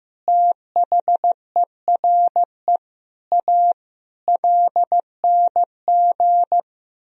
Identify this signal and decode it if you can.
Morse code, heard as a keyed tone, about 15 wpm, 700 Hz. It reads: THERE A LNG
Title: Sentences from Top 400 Words and Abbreviations 15wpm